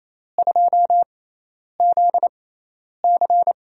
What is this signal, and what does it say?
Morse code, heard as a keyed tone, about 28 wpm, spaced as for 10 wpm, 700 Hz. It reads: 2ZC
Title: USA Call Signs - 4 Characters - 6x Character Spacing - 28wpm